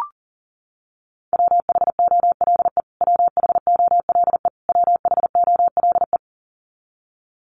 Morse code, heard as a keyed tone, about 40 wpm, 700 Hz. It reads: WHOLE WHOLE WHOLE
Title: Top 500 Words Repeated 40wpm